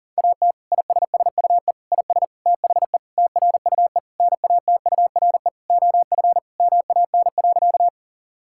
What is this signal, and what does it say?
Morse code, heard as a keyed tone, about 40 wpm, 700 Hz. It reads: AT ISSUE IS THE TRUE NATURE OF MAN.
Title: Sentences from Top 1000 Words - Review Part 6 - 40wpm